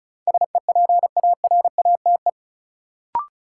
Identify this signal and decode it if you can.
Morse code, heard as a keyed tone, about 36 wpm, 700 Hz. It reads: SEPARATE
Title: Sentences from Top 1000 Words - Review Part 7 - New Word Drill - 35wpm